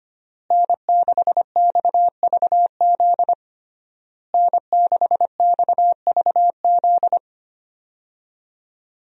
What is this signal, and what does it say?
Morse code, heard as a keyed tone, about 25 wpm, 700 Hz. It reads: N6XVZ N6XVZ
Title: USA Call Signs - 5 Characters - Repeated 3x - 25wpm